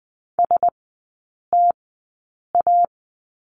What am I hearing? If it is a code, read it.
Morse code, heard as a keyed tone, about 20 wpm, spaced as for 8 wpm, 700 Hz. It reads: STA
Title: 035 Lesson 7e - Letter s - Three Characters 20wpm